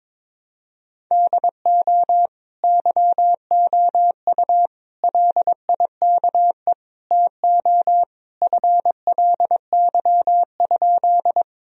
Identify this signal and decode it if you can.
Morse code, heard as a keyed tone, about 22 wpm, 700 Hz. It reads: DO YOU LIKE TO FLY?
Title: Sentences from Top 1000 Words - Review Part 2 - 22wpm